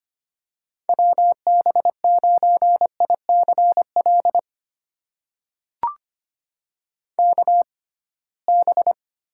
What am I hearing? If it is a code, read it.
Morse code, heard as a keyed tone, about 25 wpm, spaced as for 19 wpm, 700 Hz. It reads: WB9ICL KB
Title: USA Call Signs - 6 Characters - 6x Character Spacing - 25wpm